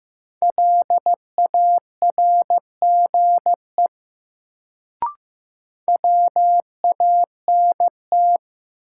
Morse code, heard as a keyed tone, about 15 wpm, 700 Hz. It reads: LARGE WANT
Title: Sentences from Top 200 Words - New Word Drill - 15wpm